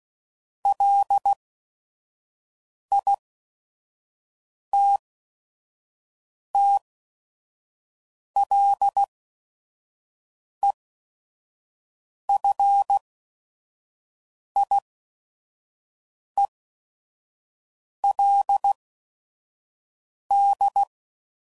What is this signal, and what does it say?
Morse code, heard as a keyed tone, about 16 wpm, spaced as for 5 wpm, 800 Hz. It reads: LITTLEFIELD
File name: CW Code Groups @5WPM / fromSM's / Cities